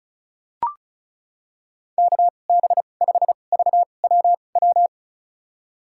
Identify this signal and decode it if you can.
Morse code, heard as a keyed tone, about 34 wpm, spaced as for 25 wpm, 700 Hz. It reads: KB5VWW
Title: USA Call Signs - 2x Character Spacing - 35wpm